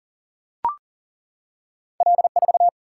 Morse code, heard as a keyed tone, about 40 wpm, 700 Hz. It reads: L4
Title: Single-Letter Single-Number 40wpm